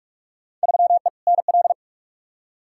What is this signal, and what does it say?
Morse code, heard as a keyed tone, about 44 wpm, 700 Hz. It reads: 3E NL
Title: Field Day Exchanges Encore - 45wpm